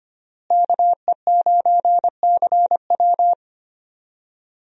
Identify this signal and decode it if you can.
Morse code, heard as a keyed tone, about 25 wpm, 700 Hz. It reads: KE9CW
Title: USA Call Signs - 5 Characters - Repeated 3x - 25wpm